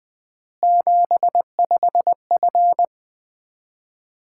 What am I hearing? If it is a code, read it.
Morse code, heard as a keyed tone, about 20 wpm, 700 Hz. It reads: 75F
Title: Common QSO Phrases - 3x Word-Spacing - 20wpm